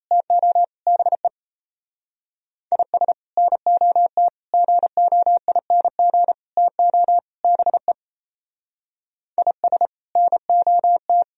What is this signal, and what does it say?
Morse code, heard as a keyed tone, about 36 wpm, 700 Hz. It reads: TO BE IS NOT GOING TO BE IS NOT